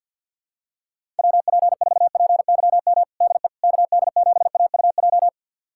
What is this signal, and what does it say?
Morse code, heard as a keyed tone, about 49 wpm, 700 Hz. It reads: WP4PYM DE KD7AUJ
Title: USA Callsign DE Callsign - 6 Characters - 50wpm